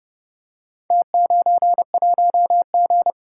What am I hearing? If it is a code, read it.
Morse code, heard as a keyed tone, about 30 wpm, 700 Hz. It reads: T91G